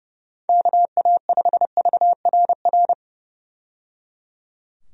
Morse code, heard as a keyed tone, about 30 wpm, 700 Hz. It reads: KA5VRR
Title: USA Call Signs - 6 Characters - 30wpm